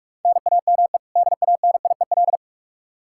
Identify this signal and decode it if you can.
Morse code, heard as a keyed tone, about 46 wpm, 700 Hz. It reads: NAME DANIEL
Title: Practice QSO - Exchange 1 - Beginning - 45wpm